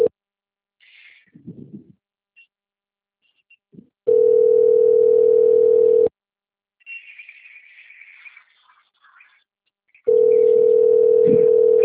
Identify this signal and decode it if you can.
Ringback tone (North America), 440 + 480 Hz.
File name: out- - - - - .31821